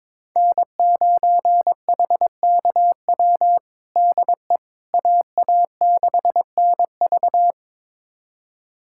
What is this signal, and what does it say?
Morse code, heard as a keyed tone, about 22 wpm, 700 Hz. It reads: N9HKW DE AA6NV
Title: USA Callsign DE Callsign - 5 Characters - 22wpm